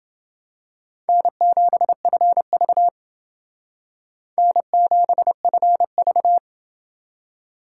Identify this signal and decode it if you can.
Morse code, heard as a keyed tone, about 28 wpm, 700 Hz. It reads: N7FV N7FV